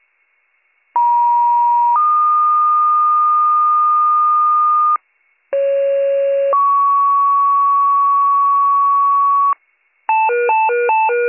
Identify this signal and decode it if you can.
Two-tone page, heard as a one-second tone, then a three-second tone, about 950 then 1220 Hz, about 550 then 1120 Hz. It is heard from 0.9 to 9.5 s.